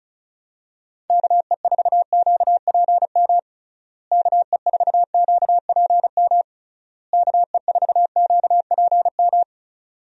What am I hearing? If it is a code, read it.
Morse code, heard as a keyed tone, about 36 wpm, 700 Hz. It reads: KE4QPM KE4QPM KE4QPM
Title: USA Call Signs - 6 Characters - Repeated 3x - 35wpm